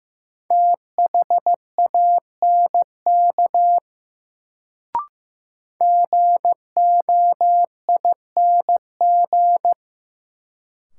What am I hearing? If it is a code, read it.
Morse code, heard as a keyed tone, about 15 wpm, 700 Hz. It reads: THANK GOING
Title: Sets of 2 Words - New Word Drill - 15wpm